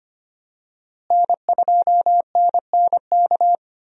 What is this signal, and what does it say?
Morse code, heard as a keyed tone, about 25 wpm, 700 Hz. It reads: N2NNK
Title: USA Call Signs - 5 Characters - Repeated 2x - 25wpm